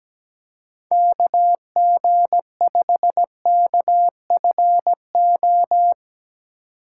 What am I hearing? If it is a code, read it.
Morse code, heard as a keyed tone, about 17 wpm, 700 Hz. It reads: KG5KFO
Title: USA Call Signs - 6 Characters - 6x Character Spacing - 17wpm